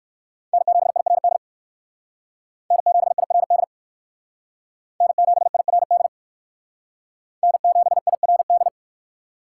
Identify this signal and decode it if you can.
Morse code, heard as a keyed tone, about 58 wpm, 700 Hz. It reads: N7IRD N7IRD N7IRD N7IRD